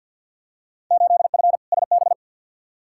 Morse code, heard as a keyed tone, about 50 wpm, 700 Hz. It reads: 8F SB
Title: Field Day Exchanges Encore - 50wpm